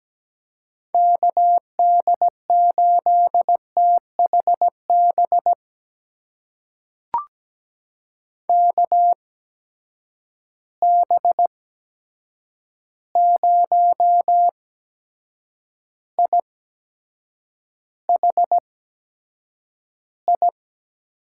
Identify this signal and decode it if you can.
Morse code, heard as a keyed tone, about 17 wpm, spaced as for 7 wpm, 700 Hz. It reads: KD8THB KB0IHI
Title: USA Call Signs - 6 Characters - 8x Character Spacing - 17wpm